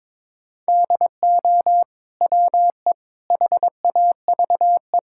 Morse code, heard as a keyed tone, about 22 wpm, 700 Hz. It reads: DO WE HAVE
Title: Sentences from Top 1000 Words - Review Part 5 - 22wpm